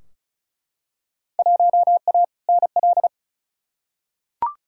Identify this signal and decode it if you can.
Morse code, heard as a keyed tone, about 36 wpm, 700 Hz. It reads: 1A NL
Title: Field Day Exchanges Encore - 35wpm